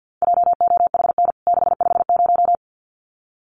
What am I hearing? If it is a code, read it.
Morse code, heard as a keyed tone, about 50 wpm, 700 Hz. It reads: JOHN 650
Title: CWT Exchange 50wpm